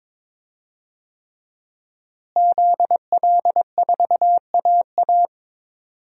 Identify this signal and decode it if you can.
Morse code, heard as a keyed tone, about 22 wpm, 700 Hz. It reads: ZL4AA
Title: DXCC-100 Call Signs - 5 Characters - Repeated 3x - 22wpm